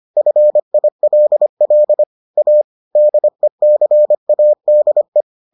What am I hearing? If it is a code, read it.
Morse code, heard as a keyed tone, about 25 wpm, 600 Hz. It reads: FILL A DECADE